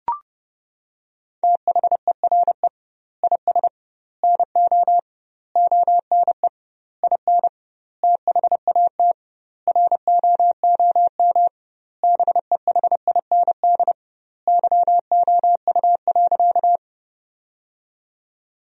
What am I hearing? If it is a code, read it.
Morse code, heard as a keyed tone, about 30 wpm, 700 Hz. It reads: THERE IS NO ONE IN THAT ROOM BEHIND YOU.
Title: 2x Word-Spacing - Sentences from Top 500 Words - 30wpm